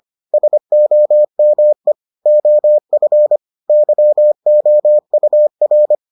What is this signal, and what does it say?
Morse code, heard as a keyed tone, about 25 wpm, 600 Hz. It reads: SOME OF YOUR